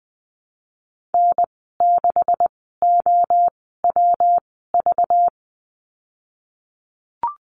Morse code, heard as a keyed tone, about 20 wpm, spaced as for 14 wpm, 700 Hz. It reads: N6OWV
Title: 207 Lesson 33g - Letter j - Call Signs 20wpm